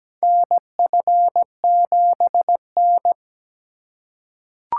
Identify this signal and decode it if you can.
Morse code, heard as a keyed tone, about 17 wpm, 700 Hz. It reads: NF7N
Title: USA Call Signs - 6x Character Spacing - 17wpm